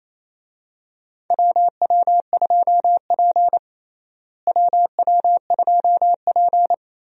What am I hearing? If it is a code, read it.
Morse code, heard as a keyed tone, about 28 wpm, 700 Hz. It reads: WW2P WW2P
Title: USA Call Signs - 4 Characters - Repeated 2x - 28wpm